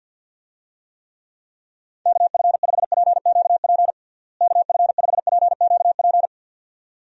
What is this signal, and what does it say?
Morse code, heard as a keyed tone, about 48 wpm, 700 Hz. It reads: KF5PQP KF5PQP
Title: USA Call Signs - 6 Characters - Repeated 2x - 50wpm